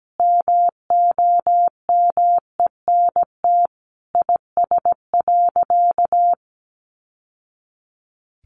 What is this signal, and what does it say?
Morse code, heard as a keyed tone, about 17 wpm, 700 Hz. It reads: MOMENT IS.
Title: Sentences from Top 1000 Words 17wpm